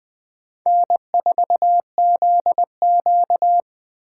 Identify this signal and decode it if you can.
Morse code, heard as a keyed tone, about 20 wpm, 700 Hz. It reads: N4ZQ